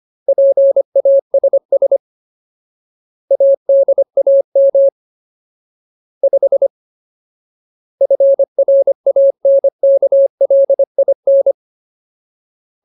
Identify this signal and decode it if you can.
Morse code, heard as a keyed tone, about 25 wpm, 550 Hz. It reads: PASS ADAM 5 FRANKLIN
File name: Ultimate Word List / 25WPM Ultimate